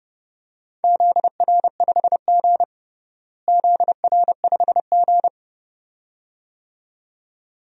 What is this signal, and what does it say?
Morse code, heard as a keyed tone, about 30 wpm, 700 Hz. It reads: ZR5G ZR5G
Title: DXCC-100 Call Signs - 4 Characters - Repeated 2x - 30wpm